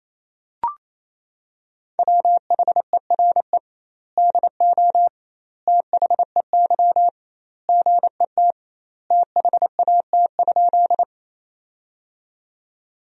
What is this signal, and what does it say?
Morse code, heard as a keyed tone, about 28 wpm, 700 Hz. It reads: WHERE DO THEY GET THAT?
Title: 2x Word-Spacing - Sentences from Top 200 Words - 28wpm